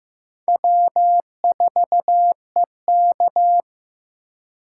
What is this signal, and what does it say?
Morse code, heard as a keyed tone, about 18 wpm, 700 Hz. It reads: W4EK
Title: Speed Racing - USA 1x2 Call Signs 15wpm